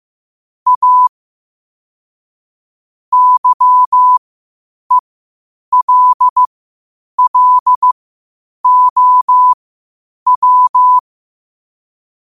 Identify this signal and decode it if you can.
Morse code, heard as a keyed tone, about 15 wpm, spaced as for 8 wpm, 1000 Hz. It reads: A YELLOW
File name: qotdfiles / 8wpm